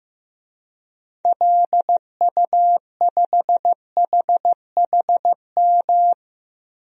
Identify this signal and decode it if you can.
Morse code, heard as a keyed tone, about 15 wpm, 700 Hz. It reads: LU5HHM
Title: DXCC-100 Call Signs - 6 Characters - 15wpm